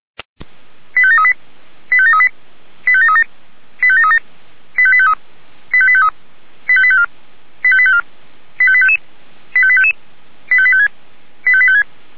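Five-tone selective call (ZVEI), heard as quick runs of five tones, tones of about 70 ms. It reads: 85738 85738 85737 85737 85853 85853 85854 85854 85855 85855 85856 85856
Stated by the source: ZVEI five-tone selective call 85723 received by SDR.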